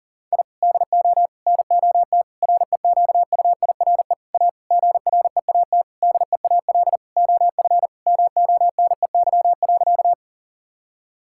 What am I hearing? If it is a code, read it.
Morse code, heard as a keyed tone, about 37 wpm, 700 Hz. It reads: I DO NOT REQUIRE A GREAT DEAL OF MONEY.
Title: Sentences from Top 1000 Words - Review Part 6 - 40wpm